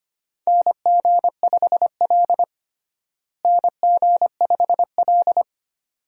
Morse code, heard as a keyed tone, about 25 wpm, 700 Hz. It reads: NG5L NG5L
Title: USA Call Signs - 4 Characters - Repeated 2x - 25wpm